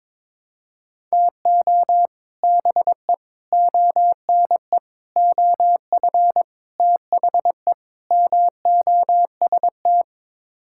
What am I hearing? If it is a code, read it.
Morse code, heard as a keyed tone, about 22 wpm, 700 Hz. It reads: TO BE ONE OF THE MOST
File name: practice / 22wpm / Sets of 6 Words - 22wpm